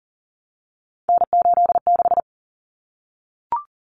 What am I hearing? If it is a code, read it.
Morse code, heard as a keyed tone, about 40 wpm, 700 Hz. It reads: N86
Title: Common Cut Numbers - 3-Digit 40wpm